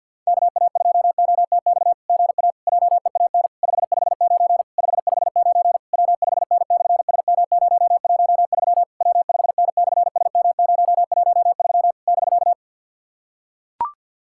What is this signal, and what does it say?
Morse code, heard as a keyed tone, about 46 wpm, 690 Hz. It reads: KA2OTX GA JEAN 559 559 W5N/SM013 W5N/SM013 <BK>
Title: SOTA QSO Practice 50wpm